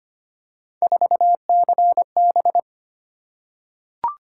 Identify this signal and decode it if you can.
Morse code, heard as a keyed tone, about 25 wpm, 700 Hz. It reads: 4CB
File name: Three Letter-Number Combinations - 25wpm